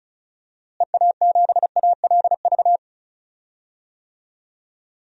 Morse code, heard as a keyed tone, about 36 wpm, 700 Hz. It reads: EA7ALV
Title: DXCC-100 Call Signs - 6 Characters - Repeated 3x - 35wpm